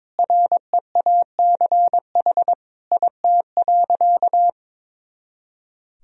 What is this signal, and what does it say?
Morse code, heard as a keyed tone, about 22 wpm, 700 Hz. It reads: REACH IT.
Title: Sentences from Top 400 Words and Abbreviations 22wpm